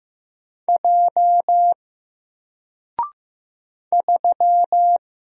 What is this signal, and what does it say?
Morse code, heard as a keyed tone, about 15 wpm, 700 Hz. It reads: J 3
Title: Single Letter-Number 15wpm